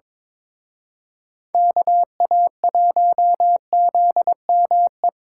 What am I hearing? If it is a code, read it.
Morse code, heard as a keyed tone, about 26 wpm, 700 Hz. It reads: KA1ZME
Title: Speed Racing - USA 2x3 Call Signs 22wpm